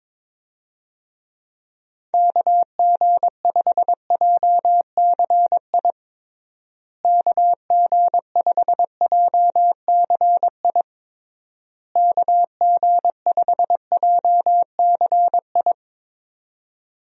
Morse code, heard as a keyed tone, about 22 wpm, 700 Hz. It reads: KG5JCI KG5JCI KG5JCI
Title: USA Call Signs - 6 Characters - Repeated 3x - 22wpm